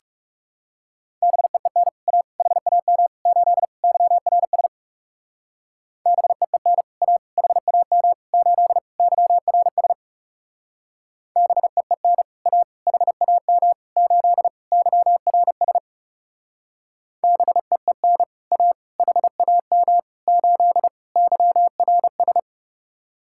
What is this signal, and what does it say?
Morse code, heard as a keyed tone, about 38 wpm, 700 Hz. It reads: BEEN A HAM 8 YRS BEEN A HAM 8 YRS BEEN A HAM 8 YRS BEEN A HAM 8 YRS